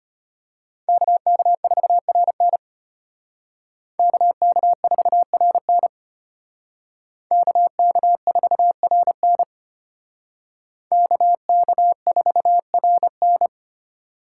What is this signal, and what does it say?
Morse code, heard as a keyed tone, about 32 wpm, 700 Hz. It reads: KK4RN KK4RN KK4RN KK4RN